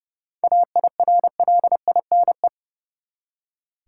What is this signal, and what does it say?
Morse code, heard as a keyed tone, about 30 wpm, 700 Hz. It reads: AIRLINE
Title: Seven Letter Words - 30wpm